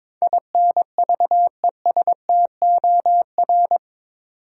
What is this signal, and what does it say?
Morse code, heard as a keyed tone, about 22 wpm, 700 Hz. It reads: INVESTOR
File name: practice / 22wpm / Eight Letter Words - 22wpm